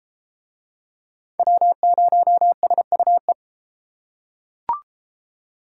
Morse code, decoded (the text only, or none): W0SUE